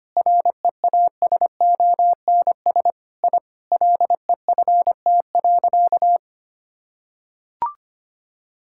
Morse code, heard as a keyed tone, about 25 wpm, 700 Hz. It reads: REASONS I LEFT.
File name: practice / 25wpm / Sentences with 4-Word Phrases - 25wpm